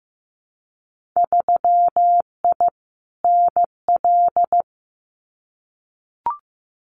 Morse code, heard as a keyed tone, about 15 wpm, 700 Hz. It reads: 3I NL
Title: Winter Field Day Exchanges 15wpm